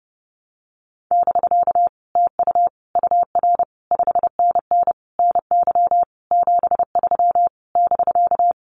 Morse code, heard as a keyed tone, about 30 wpm, 700 Hz. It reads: <BK> TU UR 5NN NY 73 <BK>
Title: Common QSO Phrases Encore 30wpm